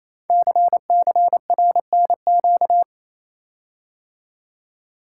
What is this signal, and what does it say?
Morse code, heard as a keyed tone, about 29 wpm, 700 Hz. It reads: CCRNQ
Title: Code Groups - 5 Letter:Numbers - 28wpm